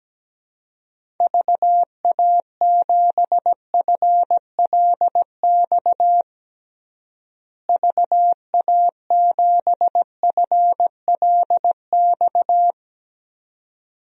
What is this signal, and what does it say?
Morse code, heard as a keyed tone, about 17 wpm, 700 Hz. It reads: VA7FLX VA7FLX
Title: DXCC-100 Call Signs - 6 Characters - Repeated 2x - 17wpm